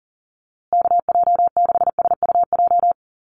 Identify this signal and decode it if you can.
Morse code, heard as a keyed tone, about 40 wpm, 700 Hz. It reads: KJ6SUJ
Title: USA Call Signs Repeated 2x 40wpm